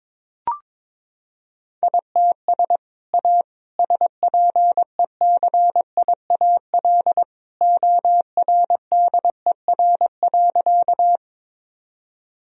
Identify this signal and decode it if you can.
Morse code, heard as a keyed tone, about 22 wpm, 700 Hz. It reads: ITS A SPECIAL ORDER.
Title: Sentences from Top 1000 Words - Review Part 3 - 22wpm